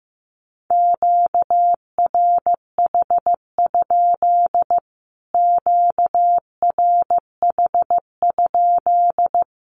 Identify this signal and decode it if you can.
Morse code, heard as a keyed tone, about 15 wpm, 700 Hz. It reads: QRH? QRH?
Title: Q-Codes Repeated 3x 15wpm